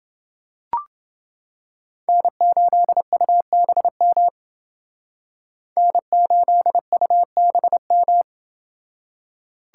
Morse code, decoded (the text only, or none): N8UBM N8UBM